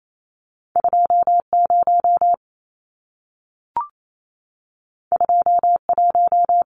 Morse code, heard as a keyed tone, about 28 wpm, 700 Hz. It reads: 20 21
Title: Numbers - 2-Digit 28wpm